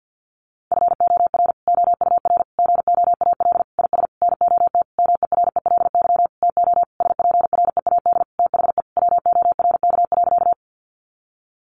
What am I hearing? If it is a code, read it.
Morse code, heard as a keyed tone, about 48 wpm, 700 Hz. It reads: FOR OUR GOAL IS NOT MERELY TO SPREAD THE WORK.
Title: Sentences from Top 1400 Words 50wpm